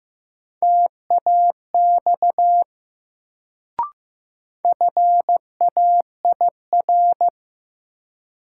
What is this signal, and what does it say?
Morse code, heard as a keyed tone, about 15 wpm, 700 Hz. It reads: TAX FAIR
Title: Sentences from Top 1000 Words - Review Part 5 - New Word Drill - 15wpm